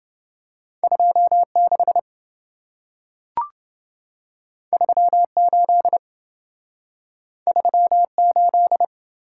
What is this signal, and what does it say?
Morse code, heard as a keyed tone, about 27 wpm, 700 Hz. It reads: 26 38 38